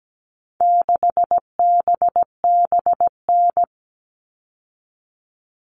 Morse code, heard as a keyed tone, about 17 wpm, 700 Hz. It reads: 6BBN
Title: All Cut Numbers - 4-Digit 17wpm